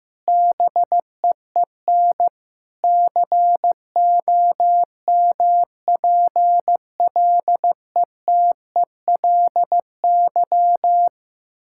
Morse code, heard as a keyed tone, about 15 wpm, 700 Hz. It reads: BEEN COMPLETELY
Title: Sentences from Top 1600 Words 15wpm